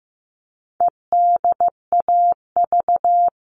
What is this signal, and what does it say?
Morse code, heard as a keyed tone, about 15 wpm, 700 Hz. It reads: EDAV
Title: All Cut Numbers - 4-Digit 15wpm